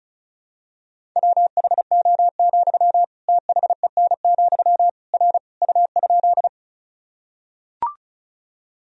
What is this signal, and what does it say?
Morse code, heard as a keyed tone, about 34 wpm, 700 Hz. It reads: WHO, THEN, R U?
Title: Sentences from Top 100 Words and Abbreviations 35wpm